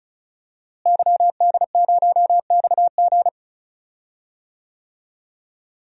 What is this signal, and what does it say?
Morse code, heard as a keyed tone, about 34 wpm, 700 Hz. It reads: YD0XG
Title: DXCC-100 Call Signs - 5 Characters - Repeated 2x - 35wpm